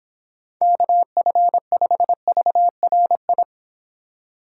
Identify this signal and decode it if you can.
Morse code, heard as a keyed tone, about 29 wpm, 700 Hz. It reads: KF5VRI